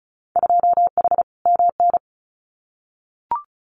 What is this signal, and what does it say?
Morse code, heard as a keyed tone, about 34 wpm, 700 Hz. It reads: 2H MN
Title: Winter Field Day Exchanges 35wpm